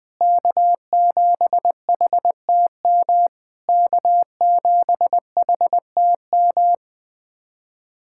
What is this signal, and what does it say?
Morse code, heard as a keyed tone, about 20 wpm, 700 Hz. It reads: K7HTM K7HTM
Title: POTA QSO Practice - 20wpm